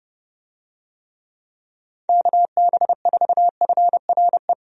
Morse code, heard as a keyed tone, about 30 wpm, 700 Hz. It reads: KB4FRE
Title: USA Call Signs - 6 Characters - 30wpm